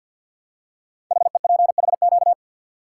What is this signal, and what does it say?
Morse code, heard as a keyed tone, about 49 wpm, 700 Hz. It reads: HEPHQ
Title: Code Groups - 5 Letters - 50wpm